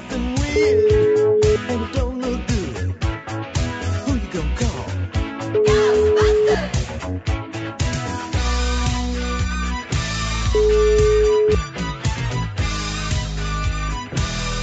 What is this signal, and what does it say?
Ringback tone (Europe, 425 Hz), with speech.